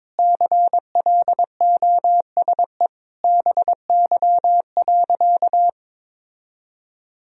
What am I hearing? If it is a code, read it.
Morse code, heard as a keyed tone, about 22 wpm, 700 Hz. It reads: CLOSE BY.
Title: Sentences from Top 1000 Words - Review Part 1 - 22wpm